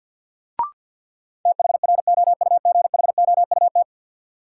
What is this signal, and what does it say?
Morse code, heard as a keyed tone, about 53 wpm, 700 Hz. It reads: THROUGHOUT
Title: Sentences from Top 800 Words - New Word Drill - 50wpm